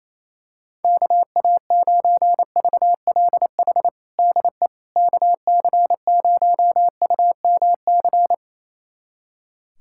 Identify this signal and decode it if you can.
Morse code, heard as a keyed tone, about 28 wpm, 700 Hz. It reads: KA9VLH DE KC0UMC
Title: USA Callsign DE Callsign - 6 Characters - 28wpm